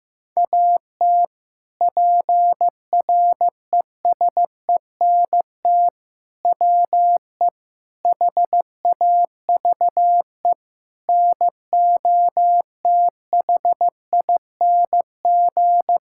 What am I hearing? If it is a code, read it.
Morse code, heard as a keyed tone, about 16 wpm, 700 Hz. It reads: AT PRESENT WE HAVE NOTHING
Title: Sentences from Top 1000 Words - Review Part 5 - 15wpm